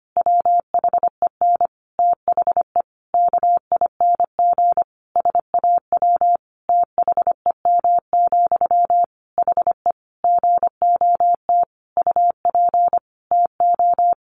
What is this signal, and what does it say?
Morse code, heard as a keyed tone, about 25 wpm, 700 Hz. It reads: WHEN THE KING SAW THEM, HE GOT UP TO
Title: Sentences from Top 1100 Words 25wpm